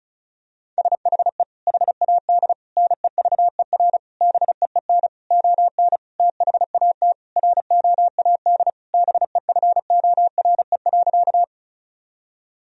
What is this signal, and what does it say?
Morse code, heard as a keyed tone, about 36 wpm, 700 Hz. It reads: SHE HAD NEVER BEEN ON THAT ROAD BEFORE.
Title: Sentences from Top 1000 Words - Review Part 2 - 35wpm